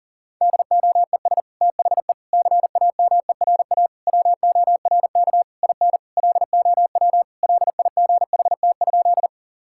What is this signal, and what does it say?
Morse code, heard as a keyed tone, about 40 wpm, 700 Hz. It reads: DOES THE CAMERA WORK IN LOW LIGHT?
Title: Sentences from Top 1000 Words - Review Part 5 - 40wpm